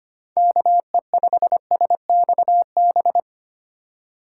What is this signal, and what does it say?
Morse code, heard as a keyed tone, about 25 wpm, 700 Hz. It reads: KE5SXB
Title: USA Call Signs - 6 Characters - 4x Character Spacing - 25wpm